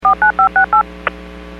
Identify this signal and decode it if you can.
DTMF, heard as telephone touch tones, tones of about 110 ms. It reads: 49568
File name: WPS / 06543 14-36 WPS 451.25 WPS